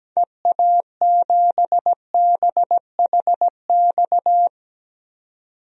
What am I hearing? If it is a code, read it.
Morse code, heard as a keyed tone, about 17 wpm, 700 Hz. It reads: EA7BHX
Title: DXCC-100 Call Signs - 6 Characters - 17wpm